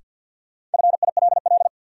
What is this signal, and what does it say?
Morse code, heard as a keyed tone, about 47 wpm, 700 Hz. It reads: FILL
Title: Sentences from Top 1000 Words - Review Part 3 - New Word Drill - 50wpm